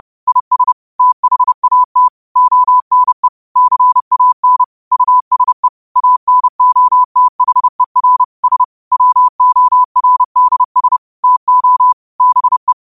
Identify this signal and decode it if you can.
Morse code, heard as a keyed tone, about 30 wpm, 1000 Hz. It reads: IS THAT ONE CAN USE ANOTHER S WORDS TO BE